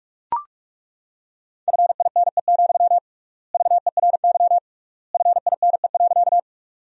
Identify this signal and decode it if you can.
Morse code, heard as a keyed tone, about 42 wpm, 700 Hz. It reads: FINE, VERY FINE.
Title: 3x Word-Spacing - Sentences from Top 500 Words - 45wpm